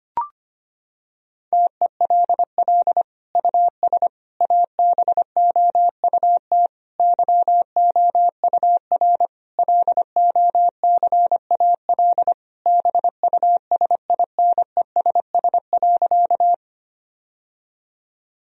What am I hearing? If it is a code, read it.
Morse code, heard as a keyed tone, about 25 wpm, 700 Hz. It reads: TELL US ABOUT YOUR LOCAL BUSINESS.
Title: Sentences from Top 1000 Words - Review Part 3 - 25wpm